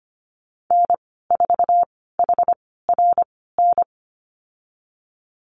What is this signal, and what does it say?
Morse code, heard as a keyed tone, about 25 wpm, spaced as for 16 wpm, 700 Hz. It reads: N4HRN
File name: Lesson 15g - Number 5 - Call Signs 25wpm